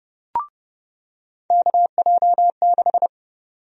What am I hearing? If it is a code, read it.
Morse code, heard as a keyed tone, about 30 wpm, 700 Hz. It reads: KJ6
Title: USA Call Sign Prefixes - 30wpm